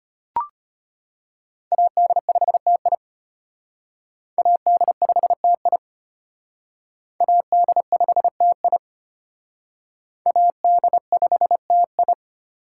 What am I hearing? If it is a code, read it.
Morse code, heard as a keyed tone, about 34 wpm, 700 Hz. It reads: AD5TI AD5TI AD5TI AD5TI